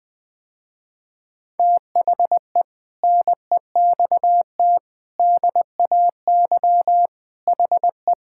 Morse code, heard as a keyed tone, about 20 wpm, 700 Hz. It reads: THE NEXT DAY HE
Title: Sentences with 3-Word Phrases - Spoken Beginning - 20wpm